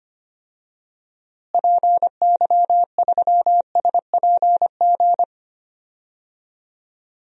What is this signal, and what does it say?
Morse code, heard as a keyed tone, about 25 wpm, 700 Hz. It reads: PY3SPG
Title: DXCC-100 Call Signs - 6 Characters - Repeated 2x - 25wpm